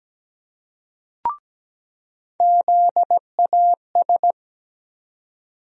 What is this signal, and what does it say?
Morse code, heard as a keyed tone, about 17 wpm, 700 Hz. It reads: ZAS
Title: USA Call Sign Postfixes - 3 Letters 17wpm